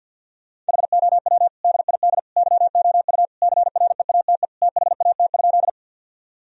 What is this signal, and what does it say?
Morse code, heard as a keyed tone, about 50 wpm, 700 Hz. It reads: HOW DID YOU CREATE THAT?
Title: Sentences from Top 1000 Words - Review Part 3 - 50wpm